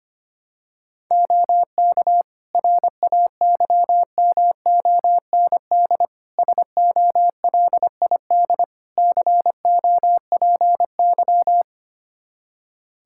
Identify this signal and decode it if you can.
Morse code, heard as a keyed tone, about 25 wpm, 700 Hz. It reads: OK RAYMOND SOLID COPY
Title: Practice QSO - Exchange 3 - More Info - 25wpm